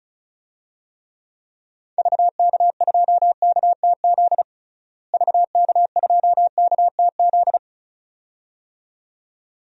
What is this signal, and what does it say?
Morse code, heard as a keyed tone, about 34 wpm, 700 Hz. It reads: VK2KTZ VK2KTZ
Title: DXCC-100 Call Signs - 6 Characters - Repeated 2x - 35wpm